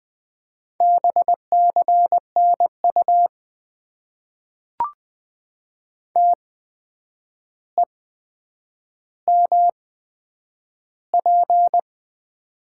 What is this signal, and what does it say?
Morse code, heard as a keyed tone, about 20 wpm, spaced as for 9 wpm, 700 Hz. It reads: BCNU TEMP
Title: Top 90 QSO Elements - 8x Character Spacing - 20wpm